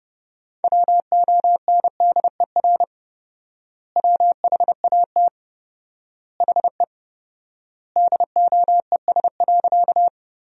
Morse code, heard as a keyed tone, about 32 wpm, 700 Hz. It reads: WONDER WHAT HE DOES.
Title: 4x Word-Spacing - Sentences from Top 500 Words - 30wpm